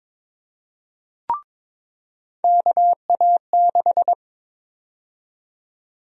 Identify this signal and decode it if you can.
Morse code, heard as a keyed tone, about 22 wpm, 700 Hz. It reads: KA6
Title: USA Call Sign Prefixes - 22wpm